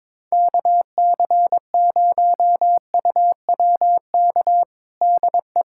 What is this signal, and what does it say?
Morse code, heard as a keyed tone, about 22 wpm, 700 Hz. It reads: KC0UWK DE
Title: USA Callsign DE Callsign - 6 Characters - 22wpm